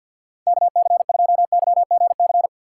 Morse code, heard as a keyed tone, about 49 wpm, 700 Hz. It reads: KC2YGC
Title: USA Call Signs - 6 Characters - 4x Character Spacing - 50wpm